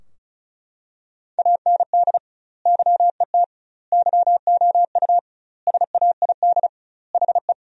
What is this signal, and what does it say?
Morse code, heard as a keyed tone, about 34 wpm, 700 Hz. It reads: AND YET YOU SAID HE
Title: 2x Word-Spacing - Sentences from Top 500 Words - 35wpm